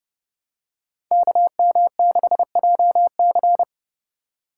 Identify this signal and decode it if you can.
Morse code, heard as a keyed tone, about 30 wpm, 700 Hz. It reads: KM6JC
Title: Code Groups - 5 Letter:Numbers - 30wpm